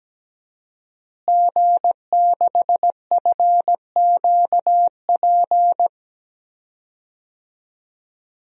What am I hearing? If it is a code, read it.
Morse code, heard as a keyed tone, about 17 wpm, 700 Hz. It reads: G6FQP
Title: DXCC-100 Call Signs - 5 Characters - Repeated 3x - 17wpm